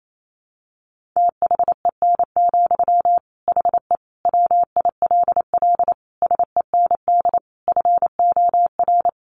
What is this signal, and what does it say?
Morse code, heard as a keyed tone, about 28 wpm, 700 Hz. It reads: THEN, HE WILL SEND FOR
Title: Sentences from Top 1500 Words 28wpm